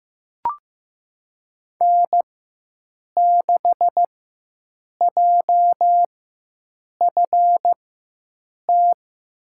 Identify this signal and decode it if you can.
Morse code, heard as a keyed tone, about 15 wpm, spaced as for 7 wpm, 700 Hz. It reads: N6JFT
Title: USA Call Signs - 4x Character Spacing - 15wpm